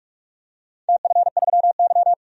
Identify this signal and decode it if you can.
Morse code, heard as a keyed tone, about 43 wpm, 700 Hz. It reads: TF3Y
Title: DXCC-100 Call Signs - 4 Characters - 45wpm